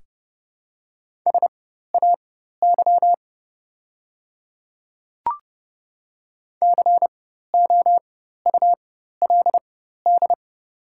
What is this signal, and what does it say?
Morse code, heard as a keyed tone, about 30 wpm, spaced as for 14 wpm, 700 Hz. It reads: SAY COULD
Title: 4x Character Spacing - Top 300 Words - 30wpm